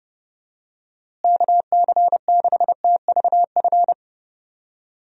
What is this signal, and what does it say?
Morse code, heard as a keyed tone, about 30 wpm, 700 Hz. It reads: KC6TVF
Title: USA Call Signs - 6 Characters - Repeated 2x - 30wpm